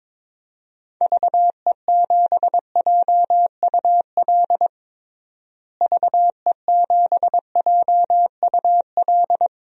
Morse code, heard as a keyed tone, about 22 wpm, 700 Hz. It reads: VE7JUL VE7JUL
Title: DXCC-100 Call Signs - 6 Characters - Repeated 3x - 22wpm